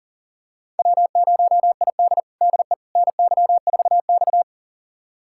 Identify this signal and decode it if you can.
Morse code, heard as a keyed tone, about 40 wpm, 700 Hz. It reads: W0ID DE NY4X
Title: USA Callsign DE Callsign - 4 Characters - 40wpm